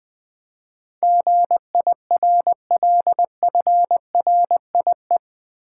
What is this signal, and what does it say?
Morse code, heard as a keyed tone, about 20 wpm, 700 Hz. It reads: GIRLFRIE
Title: Build A Word - 4 Letters At A Time - 10-Letter Words - 20wpm